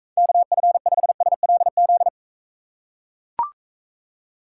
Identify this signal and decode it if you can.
Morse code, heard as a keyed tone, about 44 wpm, 700 Hz. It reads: KF5SLZ